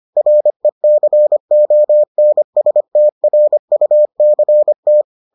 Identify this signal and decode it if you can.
Morse code, heard as a keyed tone, about 25 wpm, 600 Hz. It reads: RECONSTRUCT